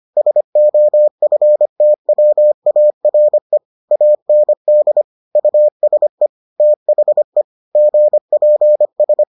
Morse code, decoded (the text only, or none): SOFTWARE AND USE THE GPS